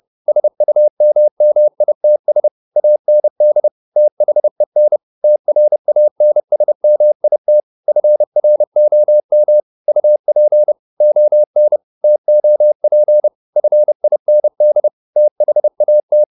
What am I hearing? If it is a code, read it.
Morse code, heard as a keyed tone, about 30 wpm, 600 Hz. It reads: SUMMITS AND THEN TRANSMIT FROM UP ON TOP FIND THAT